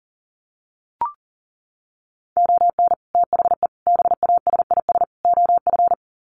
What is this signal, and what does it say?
Morse code, heard as a keyed tone, about 40 wpm, 700 Hz. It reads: ON THE BASIS OF
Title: Sets of 4 Words 40wpm